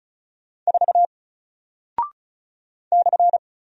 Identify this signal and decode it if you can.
Morse code, heard as a keyed tone, about 36 wpm, 700 Hz. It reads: 4 /
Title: Single Character (Letter:Number:Punctuation:Prosign) Alt 35wpm